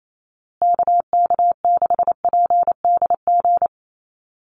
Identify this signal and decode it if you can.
Morse code, heard as a keyed tone, about 28 wpm, 700 Hz. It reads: KK6PDG
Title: USA Call Signs Repeated 3x 28wpm